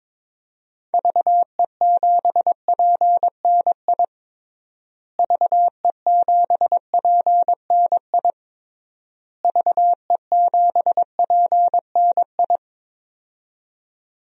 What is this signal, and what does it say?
Morse code, heard as a keyed tone, about 22 wpm, 700 Hz. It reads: VE7PNI VE7PNI VE7PNI
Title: DXCC-100 Call Signs - 6 Characters - Repeated 3x - 22wpm